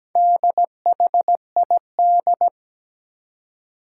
Morse code, heard as a keyed tone, about 17 wpm, 700 Hz. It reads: DHID
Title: Four Challenging Letter-Number Combinations 17wpm